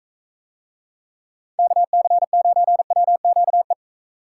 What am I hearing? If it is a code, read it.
Morse code, heard as a keyed tone, about 44 wpm, 700 Hz. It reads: KC9WQE